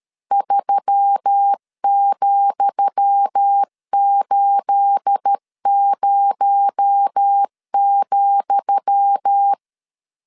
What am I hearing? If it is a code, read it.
Morse code, heard as a keyed tone, about 13 wpm, 800 Hz. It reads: 3,80,
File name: LEZIONI / 10 SIGNS / 10D COMA 12wpm 2544